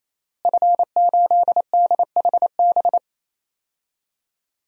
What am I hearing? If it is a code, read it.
Morse code, heard as a keyed tone, about 28 wpm, 700 Hz. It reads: F8DHB
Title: DXCC-100 Call Signs - 5 Characters - Repeated 3x - 28wpm